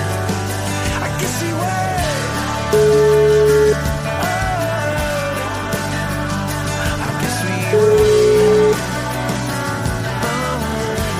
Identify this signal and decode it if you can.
Ringback tone (Europe, 425 Hz), with noise.